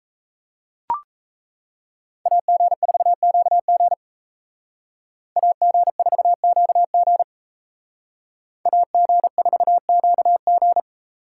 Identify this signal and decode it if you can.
Morse code, heard as a keyed tone, about 34 wpm, 700 Hz. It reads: AG4QG AG4QG AG4QG
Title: Speed Racing - USA 2x2 Call Signs 28wpm